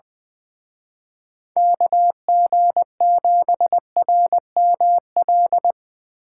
Morse code, heard as a keyed tone, about 23 wpm, 700 Hz. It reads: KG7RML
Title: Speed Racing - USA 2x3 Call Signs 20wpm